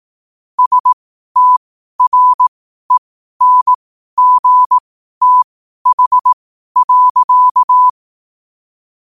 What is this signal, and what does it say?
Morse code, heard as a keyed tone, about 18 wpm, spaced as for 12 wpm, 1000 Hz. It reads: STRENGTH.
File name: qotdfiles / 12wpm